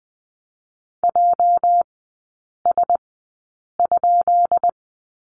Morse code, heard as a keyed tone, about 20 wpm, spaced as for 8 wpm, 690 Hz. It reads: JS?
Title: 205 Lesson 33e - Letter j - Three Characters 20wpm